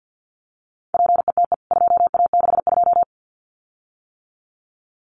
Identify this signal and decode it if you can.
Morse code, heard as a keyed tone, about 50 wpm, 700 Hz. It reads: PETE 2A62
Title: CWT Exchange 50wpm